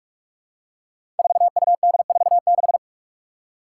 Morse code, heard as a keyed tone, about 42 wpm, 700 Hz. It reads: 4UD46
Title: Five Challenging Letter-Number Combinations 45wpm